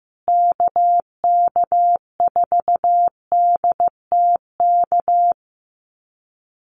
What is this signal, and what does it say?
Morse code, heard as a keyed tone, about 15 wpm, 700 Hz. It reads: KK4DTK
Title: USA Call Signs Repeated 2x 15wpm